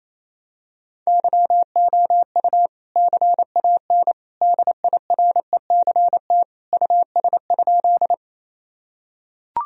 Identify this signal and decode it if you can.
Morse code, heard as a keyed tone, about 28 wpm, 700 Hz. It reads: YOU CAN DIRECT US?
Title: Sentences from Top 1000 Words - Review Part 2 - 28wpm